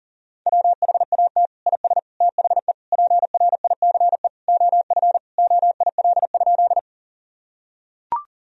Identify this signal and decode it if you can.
Morse code, heard as a keyed tone, about 37 wpm, 700 Hz. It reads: WHAT IS THE PRICE OF OIL?
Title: Sentences from Top 1000 Words - Review Part 4 - 40wpm